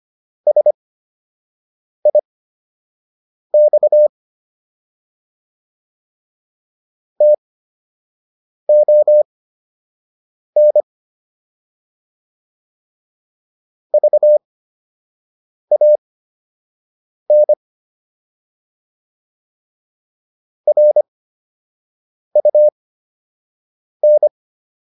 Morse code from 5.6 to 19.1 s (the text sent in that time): TON VAN